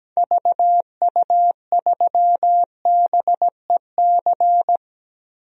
Morse code, heard as a keyed tone, about 17 wpm, 700 Hz. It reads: VU3BEC
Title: DXCC-100 Call Signs - 6 Characters - Repeated 2x - 17wpm